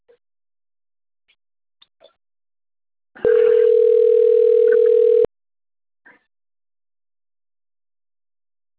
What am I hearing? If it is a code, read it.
Ringback tone (North America), 440 + 480 Hz.
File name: out- - - - - .26780